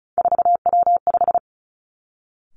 Morse code, heard as a keyed tone, about 36 wpm, 700 Hz. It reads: 4W5
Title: Three Challenging Letter-Number Combinations 35wpm